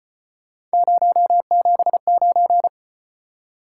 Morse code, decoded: 079

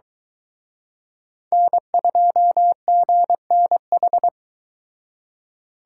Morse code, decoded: N2GNH

